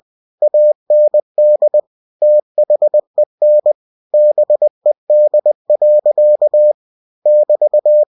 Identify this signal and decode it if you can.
Morse code, heard as a keyed tone, about 20 wpm, 600 Hz. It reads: AND THEN BED. <BT>
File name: cw-books / winnie-the-pooh / winnie-20wpm-page-0073